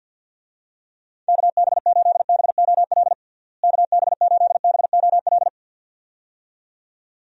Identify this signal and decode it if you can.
Morse code, heard as a keyed tone, about 52 wpm, 700 Hz. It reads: KB8BOL KB8BOL
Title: USA Call Signs - 6 Characters - Repeated 2x - 50wpm